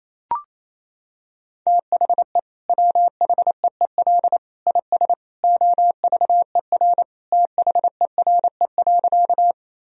Morse code, heard as a keyed tone, about 28 wpm, 700 Hz. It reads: THE WHEEL IS OVER THERE.
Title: Sentences from Top 1000 Words - Review Part 3 - 28wpm